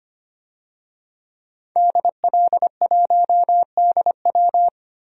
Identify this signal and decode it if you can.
Morse code, heard as a keyed tone, about 25 wpm, 700 Hz. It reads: DL1DW
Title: DXCC-100 Call Signs - 5 Characters - 25wpm